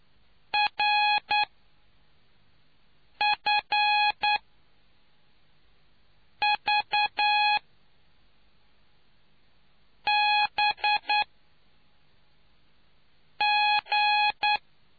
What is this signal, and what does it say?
Morse code, heard as a keyed tone, about 9 wpm, 840 Hz. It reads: R F V B G